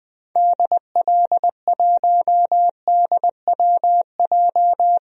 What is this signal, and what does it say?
Morse code, heard as a keyed tone, about 20 wpm, 700 Hz. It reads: DL1DWJ
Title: DXCC-100 Call Signs - 6 Characters - 20wpm